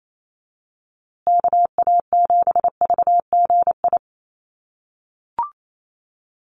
Morse code, heard as a keyed tone, about 28 wpm, 700 Hz. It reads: KA7VGI
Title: USA Call Signs Repeated 2x 28wpm